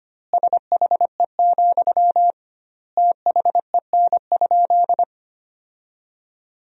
Morse code, heard as a keyed tone, about 25 wpm, 700 Hz. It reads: SHE, THEN?
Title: 2x Word-Spacing - Sentences from Top 200 Words - 25wpm